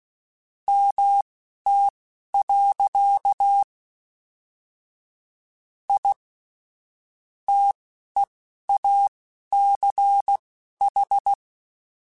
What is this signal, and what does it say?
Morse code, heard as a keyed tone, about 16 wpm, spaced as for 11 wpm, 800 Hz. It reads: MT. I TEACH